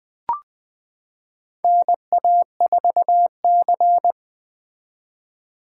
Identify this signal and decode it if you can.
Morse code, heard as a keyed tone, about 21 wpm, 700 Hz. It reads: NA4C
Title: K1USN SST QSO Practice - 20wpm